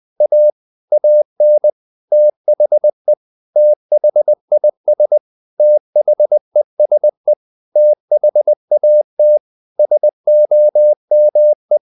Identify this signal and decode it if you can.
Morse code, heard as a keyed tone, about 20 wpm, 600 Hz. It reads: A AN THE THIS THESE THAT SOME
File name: CW / Practice / 500MostCommonEnglishWords20